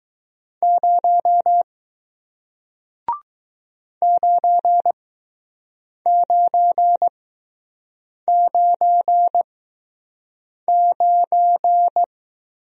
Morse code, heard as a keyed tone, about 20 wpm, spaced as for 6 wpm, 700 Hz. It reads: 0 9999